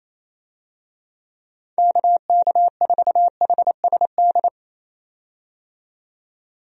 Morse code, heard as a keyed tone, about 27 wpm, 700 Hz. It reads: KK4HSD
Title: USA Call Signs - 6 Characters - 28wpm